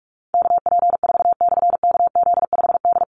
Morse code, heard as a keyed tone, about 44 wpm, 700 Hz. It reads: KP4/KZ5D
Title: Call Signs - 45wpm